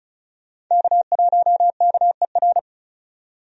Morse code, heard as a keyed tone, about 36 wpm, 700 Hz. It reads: K1KER
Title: USA Call Signs - 5 Characters - 35wpm